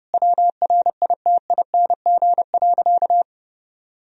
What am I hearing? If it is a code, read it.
Morse code, heard as a keyed tone, about 32 wpm, 700 Hz. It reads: WRITING.
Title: Sentences from Top 1000 Words - Review Part 1 - 30wpm